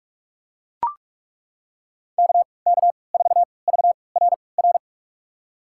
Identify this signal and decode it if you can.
Morse code, heard as a keyed tone, about 42 wpm, spaced as for 25 wpm, 700 Hz. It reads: KK4VRR
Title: USA Call Signs - 6 Characters - 3x Character Spacing - 45wpm